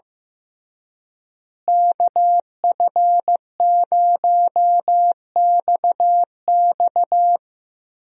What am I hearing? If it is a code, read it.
Morse code, heard as a keyed tone, about 15 wpm, 700 Hz. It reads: KF0XX
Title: USA Call Signs - 5 Characters - Repeated 2x - 15wpm